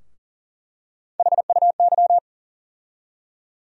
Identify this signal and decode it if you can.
Morse code, heard as a keyed tone, about 40 wpm, 700 Hz. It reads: HUY
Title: USA Call Sign Postfixes - 3 Letters 40wpm